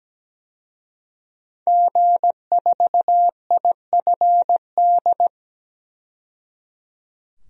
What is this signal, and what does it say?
Morse code, heard as a keyed tone, about 17 wpm, 700 Hz. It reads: G4IFD
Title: DXCC-100 Call Signs - 5 Characters - 17wpm